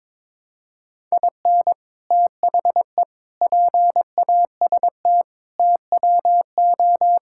Morse code, heard as a keyed tone, about 22 wpm, 700 Hz. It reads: IN THE PAST TWO
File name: practice / 22wpm / Build A Phrase - 5-Word Phrases - 22wpm